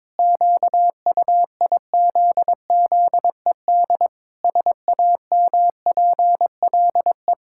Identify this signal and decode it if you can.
Morse code, heard as a keyed tone, about 22 wpm, 700 Hz. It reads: QUIZZED SAMPLE
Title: Pangram Challenge v2 - 22wpm